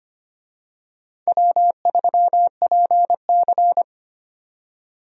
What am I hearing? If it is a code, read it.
Morse code, heard as a keyed tone, about 25 wpm, 700 Hz. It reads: W3PC
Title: USA Call Signs - 4 Characters - 25wpm